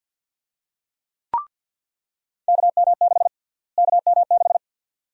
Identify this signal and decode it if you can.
Morse code, heard as a keyed tone, about 46 wpm, 700 Hz. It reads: KM6 KM6
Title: USA Call Sign Prefixes - Repeated 2x - 50wpm